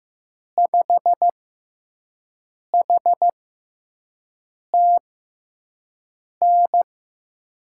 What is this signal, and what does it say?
Morse code, heard as a keyed tone, about 15 wpm, spaced as for 5 wpm, 700 Hz. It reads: 5HTN
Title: USA Call Signs - 6 Characters - 6x Character Spacing - 15wpm